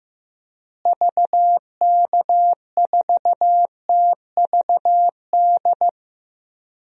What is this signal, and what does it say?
Morse code, heard as a keyed tone, about 15 wpm, 700 Hz. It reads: VK4TVD
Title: DXCC-100 Call Signs - 6 Characters - Repeated 2x - 15wpm